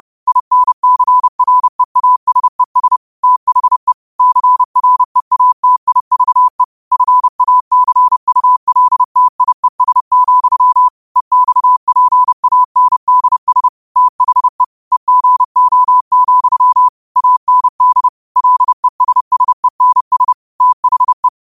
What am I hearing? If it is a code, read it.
Morse code, heard as a keyed tone, about 30 wpm, 1000 Hz. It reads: INCREASES THE CREATIVE FACULTIES, EXPANDS THE EGO, AND LESSENS THE